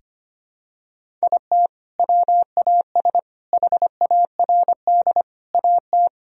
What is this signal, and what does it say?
Morse code, heard as a keyed tone, about 25 wpm, 700 Hz. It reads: IT WAS HARD AT
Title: Sentences from Top 1000 Words - Review Part 1 - 25wpm